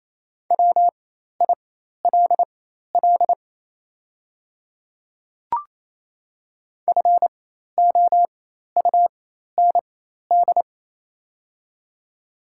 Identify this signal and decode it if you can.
Morse code, heard as a keyed tone, about 28 wpm, spaced as for 13 wpm, 700 Hz. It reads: WILL FOUND
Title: 4x Character Spacing - Top 500 Words - 28wpm